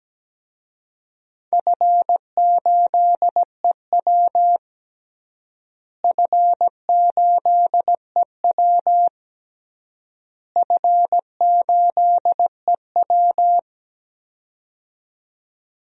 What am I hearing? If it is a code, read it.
Morse code, heard as a keyed tone, about 17 wpm, 700 Hz. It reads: F8EW F8EW F8EW
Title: DXCC-100 Call Signs - 4 Characters - Repeated 3x - 17wpm